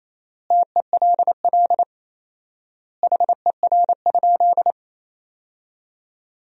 Morse code, heard as a keyed tone, about 28 wpm, 700 Hz. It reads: TELL HER?
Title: 4x Word-Spacing - Sentences from Top 200 Words - 28wpm